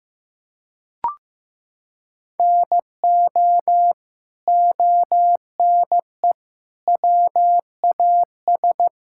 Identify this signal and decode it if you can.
Morse code, heard as a keyed tone, about 16 wpm, 700 Hz. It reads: NO ONE WAS
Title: Sentences from Top 1000 Words - Review Part 5 - 15wpm